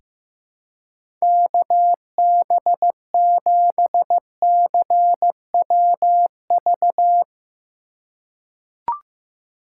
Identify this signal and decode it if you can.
Morse code, heard as a keyed tone, about 15 wpm, 700 Hz. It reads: KB7CWV
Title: USA Call Signs - 3x Character Spacing - 15wpm